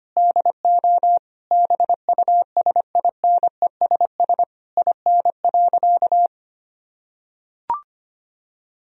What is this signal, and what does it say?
Morse code, heard as a keyed tone, about 25 wpm, 700 Hz. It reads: DO BUSINESS IN.
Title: Sentences from Top 1000 Words - Review Part 3 - 25wpm